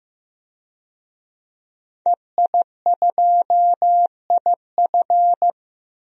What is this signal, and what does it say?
Morse code, heard as a keyed tone, about 16 wpm, 700 Hz. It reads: EI2IF
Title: DXCC-100 Call Signs - 5 Characters - 15wpm